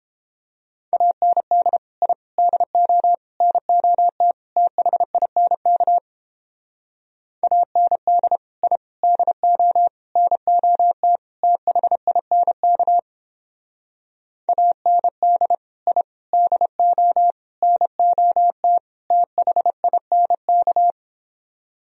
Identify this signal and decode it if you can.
Morse code, heard as a keyed tone, about 31 wpm, 700 Hz. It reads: AND I DO NOT THINK AND I DO NOT THINK AND I DO NOT THINK